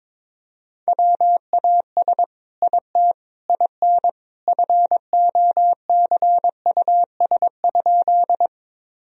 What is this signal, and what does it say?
Morse code, heard as a keyed tone, about 22 wpm, 700 Hz. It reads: WAS IT IN FOCUS?
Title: Sentences from Top 1000 Words - Review Part 4 - 22wpm